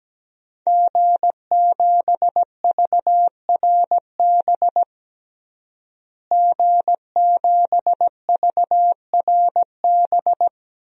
Morse code, heard as a keyed tone, about 17 wpm, 700 Hz. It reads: G7VRB G7VRB
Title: DXCC-100 Call Signs - 5 Characters - Repeated 3x - 17wpm